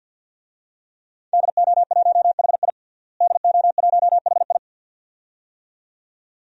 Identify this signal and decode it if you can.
Morse code, heard as a keyed tone, about 51 wpm, 700 Hz. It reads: DO1HI DO1HI
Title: DXCC-100 Call Signs - 5 Characters - Repeated 2x - 50wpm